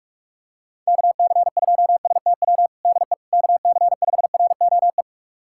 Morse code, heard as a keyed tone, about 46 wpm, 700 Hz. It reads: KC2STW DE KC5ROE
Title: USA Callsign DE Callsign - 6 Characters - 45wpm